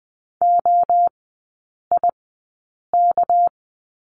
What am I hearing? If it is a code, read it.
Morse code, heard as a keyed tone, about 20 wpm, spaced as for 8 wpm, 700 Hz. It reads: OIK
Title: 212 Lesson 34e - Number 8 - Three Characters 20wpm